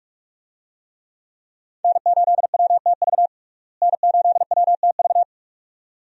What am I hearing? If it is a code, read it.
Morse code, heard as a keyed tone, about 46 wpm, 700 Hz. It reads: N8WTV N8WTV
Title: USA Call Signs - 5 Characters - Repeated 2x - 45wpm